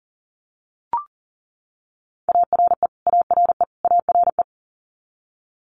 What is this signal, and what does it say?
Morse code, heard as a keyed tone, about 40 wpm, 700 Hz. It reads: ARE ARE ARE
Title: Top 500 Words Repeated 40wpm